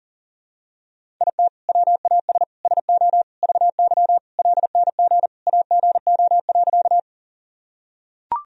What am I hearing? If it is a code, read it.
Morse code, heard as a keyed tone, about 40 wpm, 700 Hz. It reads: IT WAS SO VY LNG AGO.
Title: Sentences from Top 500 Words and Abbreviations 40wpm